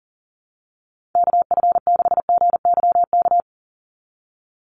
Morse code, heard as a keyed tone, about 40 wpm, 700 Hz. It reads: KF6GYK
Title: USA Call Signs 40wpm